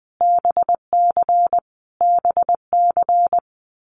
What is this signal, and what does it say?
Morse code, heard as a keyed tone, about 20 wpm, 700 Hz. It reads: BC BC
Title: Canadian Provinces and Territories - Repeated 3x - 20wpm